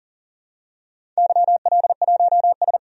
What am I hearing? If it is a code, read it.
Morse code, heard as a keyed tone, about 42 wpm, 690 Hz. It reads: YL1S
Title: DXCC-100 Call Signs - 4 Characters - 40wpm